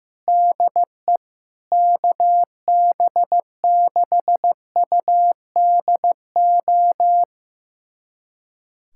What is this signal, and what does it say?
Morse code, heard as a keyed tone, about 15 wpm, 700 Hz. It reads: DE KB6UDO
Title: USA Callsign DE Callsign 15wpm